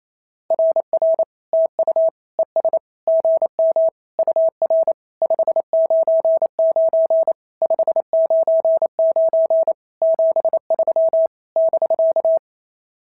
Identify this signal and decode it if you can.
Morse code, heard as a keyed tone, about 28 wpm, 650 Hz. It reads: RR TU ES GM UR 599 599 73 <BK>
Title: SOTA QSO Practice 28wpm